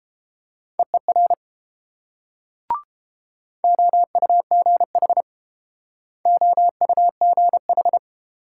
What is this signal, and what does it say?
Morse code, heard as a keyed tone, about 30 wpm, 700 Hz. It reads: EER OUGH OUGH